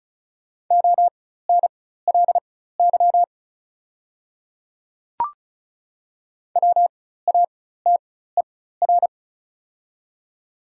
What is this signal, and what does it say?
Morse code, heard as a keyed tone, about 36 wpm, spaced as for 16 wpm, 690 Hz. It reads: ONLY WATER
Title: 4x Character Spacing - Top 200 Words - 35wpm